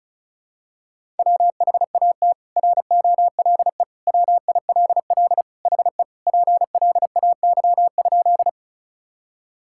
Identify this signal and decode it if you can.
Morse code, heard as a keyed tone, about 34 wpm, 700 Hz. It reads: WHAT ROLE WILL HE PLAY?
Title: Sentences from Top 1000 Words - Review Part 6 - 35wpm